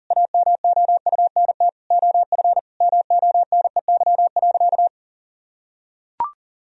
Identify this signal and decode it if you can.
Morse code, heard as a keyed tone, about 40 wpm, 700 Hz. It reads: AMOUNT OF MONEY.
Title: Sentences from Top 1000 Words - Review Part 4 - 40wpm